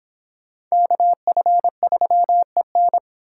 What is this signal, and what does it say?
Morse code, heard as a keyed tone, about 25 wpm, 700 Hz. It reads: KF3EN